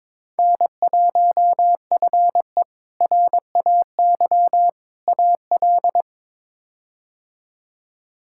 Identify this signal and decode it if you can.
Morse code, heard as a keyed tone, about 21 wpm, 700 Hz. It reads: N1FE RAY AL
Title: K1USN SST QSO Practice - 22wpm